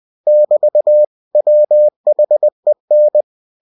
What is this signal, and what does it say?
Morse code, heard as a keyed tone, about 20 wpm, 600 Hz. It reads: <BT> WHEN
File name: cw-books / winnie-the-pooh / winnie-20wpm-page-0000